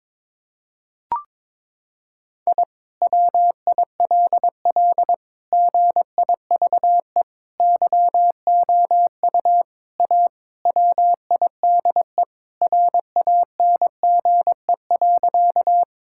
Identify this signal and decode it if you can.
Morse code, heard as a keyed tone, about 22 wpm, 700 Hz. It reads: I WILL GIVE YOU A WIDE RANGE.
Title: Sentences from Top 1000 Words - Review Part 6 - 22wpm